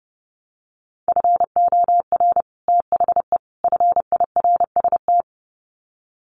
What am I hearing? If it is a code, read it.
Morse code, heard as a keyed tone, about 30 wpm, 700 Hz. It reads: FOR THE FIRST
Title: Sets of 3 Words 30wpm